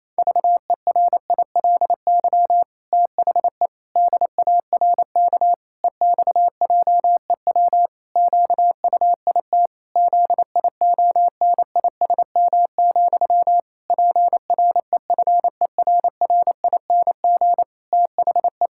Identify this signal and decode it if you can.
Morse code, heard as a keyed tone, about 28 wpm, 700 Hz. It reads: VERILY THE DARK EXJEW QUIT ZIONISM, PREFERRING THE
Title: Pangram Challenge v2 - 28wpm